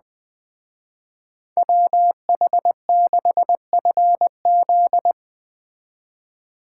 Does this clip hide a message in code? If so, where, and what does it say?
Yes: Morse code from 1.6 to 5.1 s, heard as a keyed tone, about 19 wpm, 700 Hz. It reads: WH6FZ